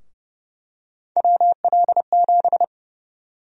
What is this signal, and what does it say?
Morse code, heard as a keyed tone, about 30 wpm, 700 Hz. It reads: WL7
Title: USA Call Sign Prefixes - 30wpm